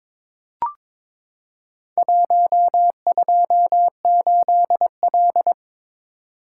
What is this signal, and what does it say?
Morse code, heard as a keyed tone, about 22 wpm, 700 Hz. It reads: 128L
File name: practice / 22wpm / Four Letter-Number Combinations - 22wpm